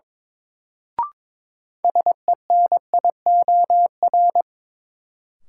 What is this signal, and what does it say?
Morse code, heard as a keyed tone, about 21 wpm, 700 Hz. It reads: SENIOR
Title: Sets of 5 Words Encore - New Word Drill - 22wpm